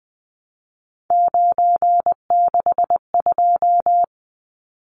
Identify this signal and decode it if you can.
Morse code, heard as a keyed tone, about 20 wpm, 700 Hz. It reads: 962
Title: Numbers - 3-Digit 20wpm